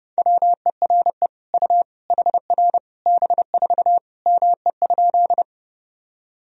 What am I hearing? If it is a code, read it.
Morse code, heard as a keyed tone, about 32 wpm, 700 Hz. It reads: WERE U HR B4 ME?
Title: Sentences from Top 200 Words and Abbreviations 30wpm